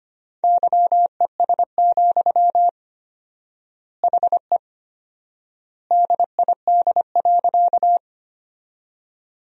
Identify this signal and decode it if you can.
Morse code, heard as a keyed tone, about 25 wpm, 700 Hz. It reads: YES, HE DID.
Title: 4x Word-Spacing - Sentences from Top 500 Words - 25wpm